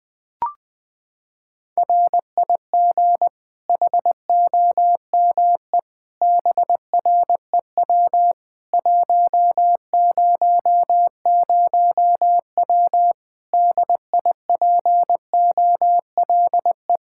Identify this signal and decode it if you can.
Morse code, heard as a keyed tone, about 20 wpm, 700 Hz. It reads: RIG HOME BREW 100W DIPOLE
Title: Practice QSO - Exchange 2 - Info - 20wpm